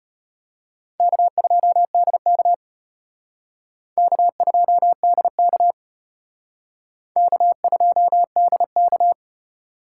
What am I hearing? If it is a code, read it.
Morse code, heard as a keyed tone, about 34 wpm, 700 Hz. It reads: K2DK K2DK K2DK